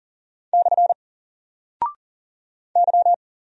Morse code, heard as a keyed tone, about 40 wpm, 700 Hz. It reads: / Y
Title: Single Character (Letter:Number:Punctuation:Prosign) Alt 40wpm